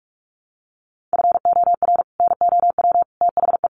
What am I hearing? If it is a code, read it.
Morse code, heard as a keyed tone, about 44 wpm, 700 Hz. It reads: FOR NOW THE
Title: Sentences from Top 200 Words 45wpm